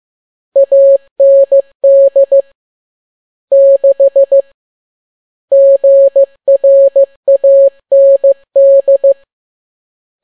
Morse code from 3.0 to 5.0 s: 6